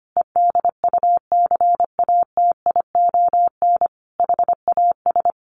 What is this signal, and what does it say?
Morse code, heard as a keyed tone, about 25 wpm, 700 Hz. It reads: EDUCATION HAS
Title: Sentences from Top 1400 Words 25wpm